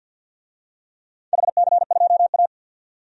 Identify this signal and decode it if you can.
Morse code, heard as a keyed tone, about 50 wpm, 700 Hz. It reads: HC2A
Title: DXCC-100 Call Signs - 4 Characters - 50wpm